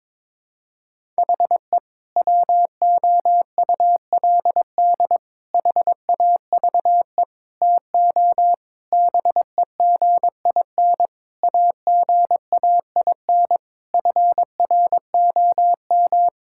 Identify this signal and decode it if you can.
Morse code, heard as a keyed tone, about 22 wpm, 700 Hz. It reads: HE WOULD HAVE TO BEGIN AGAIN FROM
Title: Sentences from Top 1000 Words - Review Part 5 - 22wpm